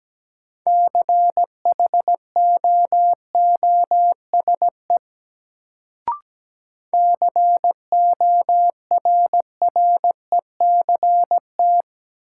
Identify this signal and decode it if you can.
Morse code, heard as a keyed tone, about 17 wpm, 700 Hz. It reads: CHOOSE CORRECT
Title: Sentences from Top 700 Words - New Word Drill - 17wpm